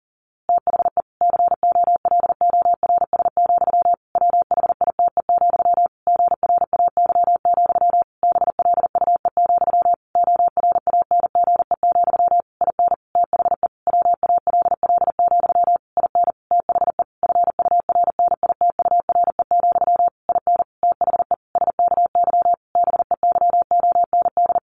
Morse code, heard as a keyed tone, about 40 wpm, 700 Hz. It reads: THE COLORS, WHITE, GRAY, BLUE, ORANGE, IN THE WALL, IN THE FURNITURE, IN THE SKY BEYOND
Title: A Sound Of Thunder 40wpm